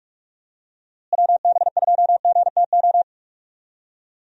Morse code, heard as a keyed tone, about 46 wpm, 700 Hz. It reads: WB2GTO
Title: USA Call Signs - 6 Characters - Repeated 3x - 45wpm